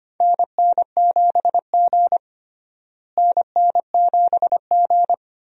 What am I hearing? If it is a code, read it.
Morse code, heard as a keyed tone, about 25 wpm, 700 Hz. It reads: NN7G NN7G
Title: USA Call Signs - 4 Characters - Repeated 2x - 25wpm